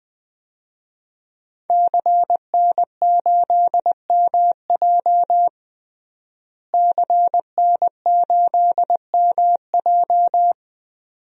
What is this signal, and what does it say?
Morse code, heard as a keyed tone, about 21 wpm, 700 Hz. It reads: CN8MJ CN8MJ
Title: DXCC-100 Call Signs - 5 Characters - Repeated 2x - 20wpm